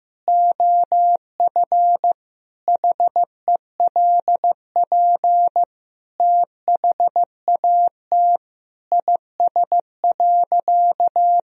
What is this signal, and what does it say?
Morse code, heard as a keyed tone, about 15 wpm, 700 Hz. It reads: OF HELP THAT IS.
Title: Sentences from Top 1000 Words - Review Part 4 - 15wpm